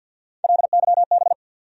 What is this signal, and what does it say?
Morse code, heard as a keyed tone, about 50 wpm, 700 Hz. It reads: LYB
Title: USA Call Sign Postfixes - 3 Letters 50wpm